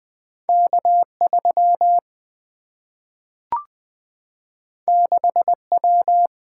Morse code, heard as a keyed tone, about 20 wpm, 700 Hz. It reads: K3 6W
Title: Single-Letter Single-Number 20wpm